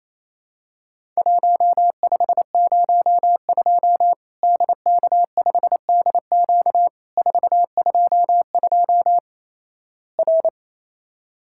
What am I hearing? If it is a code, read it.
Morse code, heard as a keyed tone, about 27 wpm, 700 Hz. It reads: 1502 DK5DQ 422 R
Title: Worked All Europe DX Contest - QTC Exchange - 28wpm